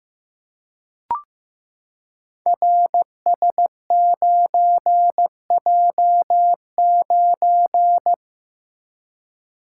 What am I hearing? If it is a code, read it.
Morse code, heard as a keyed tone, about 15 wpm, 700 Hz. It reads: RS9J9
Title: Code Groups - 5 Letter:Numbers - 15wpm